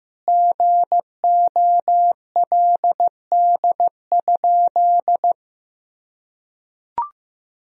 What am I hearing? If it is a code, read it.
Morse code, heard as a keyed tone, about 15 wpm, 700 Hz. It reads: GOLD?
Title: Sentences from Top 1000 Words - Review Part 3 - 15wpm